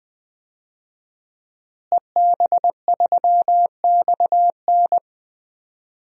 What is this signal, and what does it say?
Morse code, heard as a keyed tone, about 20 wpm, 700 Hz. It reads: EB3XN
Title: DXCC-100 Call Signs - 5 Characters - Repeated 3x - 20wpm